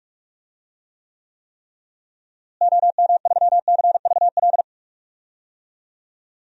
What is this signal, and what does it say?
Morse code, heard as a keyed tone, about 46 wpm, 700 Hz. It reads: OM3CVL
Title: DXCC-100 Call Signs - 6 Characters - Repeated 3x - 45wpm